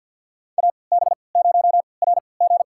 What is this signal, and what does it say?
Morse code, heard as a keyed tone, about 53 wpm, spaced as for 29 wpm, 700 Hz. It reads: AB0RG
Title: USA Call Signs - 5 Characters - 3x Character Spacing - 50wpm